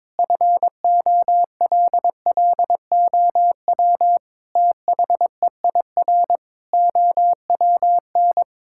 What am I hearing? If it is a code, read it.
Morse code, heard as a keyed tone, about 22 wpm, 700 Hz. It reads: FOLLOW THEIR OWN
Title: Sentences from Top 1000 Words - Review Part 6 - 22wpm